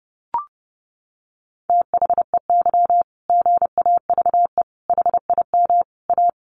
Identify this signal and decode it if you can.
Morse code, heard as a keyed tone, about 30 wpm, 700 Hz. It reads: THEY GAVE HIM A
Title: Sentences from Top 1500 Words 30wpm